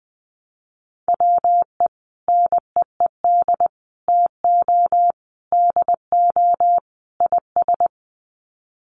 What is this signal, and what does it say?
Morse code, heard as a keyed tone, about 20 wpm, 700 Hz. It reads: WE NEED TO DO IS
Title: Sets of 5 Words Encore2 20wpm